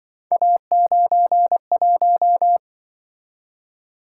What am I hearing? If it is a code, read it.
Morse code, heard as a keyed tone, about 24 wpm, 700 Hz. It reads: A91
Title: Speed-Racing - Three Letter-Number Combinations - 20wpm